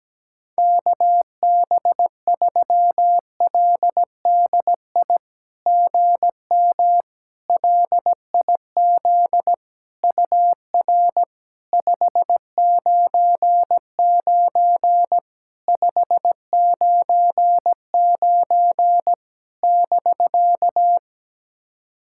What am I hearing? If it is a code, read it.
Morse code, heard as a keyed tone, about 17 wpm, 700 Hz. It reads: KB3LDI GM LIZ UR 599 599 <BK>
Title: SOTA QSO Practice 17wpm